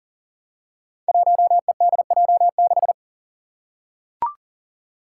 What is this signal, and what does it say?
Morse code, heard as a keyed tone, about 40 wpm, 700 Hz. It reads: 1EDJ6
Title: Five Challenging Letter-Number Combinations 40wpm